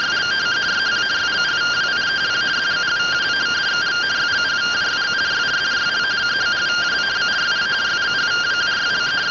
Two-tone teleprinter signal, about 170 Hz shift.